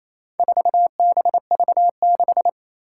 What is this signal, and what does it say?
Morse code, heard as a keyed tone, about 27 wpm, 700 Hz. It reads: 4BV6
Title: Four Challenging Letter-Number Combinations 28wpm